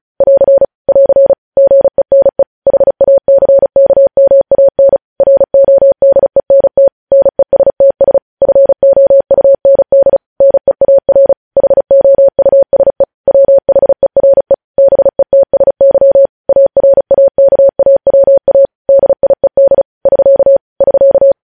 Morse code, heard as a keyed tone, about 36 wpm, 550 Hz. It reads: <AR> <AR> GENE HACKMAN RODENT NESTS FOUND NEAR HOUSE WHERE BETSY ARAKAWA DIED <SK> <SK>
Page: news headlines in morse code at 35 wpm